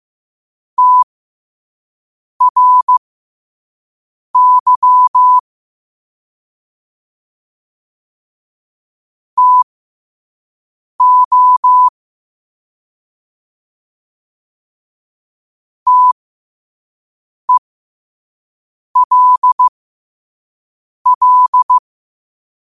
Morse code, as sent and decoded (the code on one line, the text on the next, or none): - .-. -.-- / - --- / - . .-.. .-..
TRY TO TELL